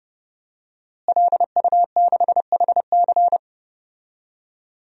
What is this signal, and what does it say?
Morse code, heard as a keyed tone, about 29 wpm, 700 Hz. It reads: LU6HC
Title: DXCC-100 Call Signs - 5 Characters - Repeated 2x - 30wpm